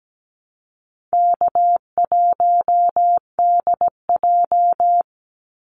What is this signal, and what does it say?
Morse code, heard as a keyed tone, about 17 wpm, 700 Hz. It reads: K1DJ
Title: Easy Call Signs Repeated 2x - 17wpm